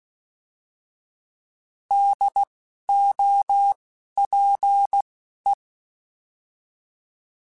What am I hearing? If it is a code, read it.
Morse code, heard as a keyed tone, about 16 wpm, spaced as for 12 wpm, 800 Hz. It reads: DOPE
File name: Alphabetical / Alphabetical mp3s / Ds @10